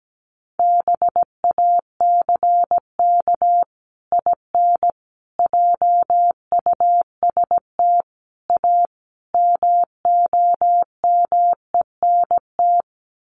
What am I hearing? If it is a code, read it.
Morse code, heard as a keyed tone, about 17 wpm, 700 Hz. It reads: BACK IN JUST A MOMENT
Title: Sets of 5 Words Encore 17wpm